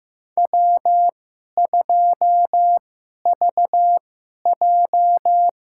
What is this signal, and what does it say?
Morse code, heard as a keyed tone, about 15 wpm, spaced as for 11 wpm, 700 Hz. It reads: W2VJ
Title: USA Call Signs - 4 Characters - 2x Character Spacing - 15wpm